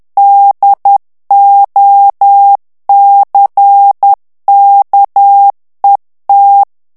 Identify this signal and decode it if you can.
Morse code, heard as a keyed tone, about 10 wpm, 800 Hz. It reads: DOCKET
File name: ANGLAIS texte clair 3 vitesse 10 mots minute